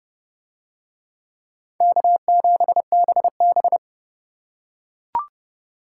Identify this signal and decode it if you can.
Morse code, heard as a keyed tone, about 29 wpm, 700 Hz. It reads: K7BB